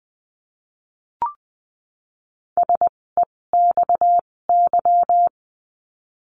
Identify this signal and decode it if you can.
Morse code, heard as a keyed tone, about 20 wpm, spaced as for 16 wpm, 700 Hz. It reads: SEXY
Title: 256 Lesson 42a - Words 12-20wpm